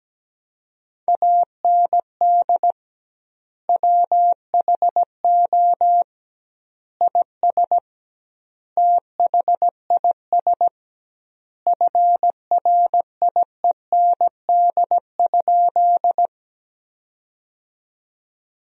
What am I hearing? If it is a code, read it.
Morse code, heard as a keyed tone, about 17 wpm, 700 Hz. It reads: AND WHO IS THIS FRIEND?
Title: 2x Word-Spacing - Sentences from Top 300 Words - 17wpm